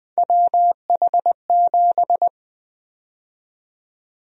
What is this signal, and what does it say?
Morse code, heard as a keyed tone, about 21 wpm, 700 Hz. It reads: WH7
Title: USA Call Sign Prefixes - 20wpm